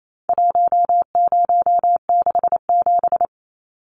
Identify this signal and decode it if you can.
Morse code, heard as a keyed tone, about 28 wpm, 700 Hz. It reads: 1067
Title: Numbers - 4-Digit 28wpm